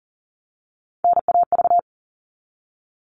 Morse code, heard as a keyed tone, about 40 wpm, 700 Hz. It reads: NAV
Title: All Cut Numbers - 3-Digit 40wpm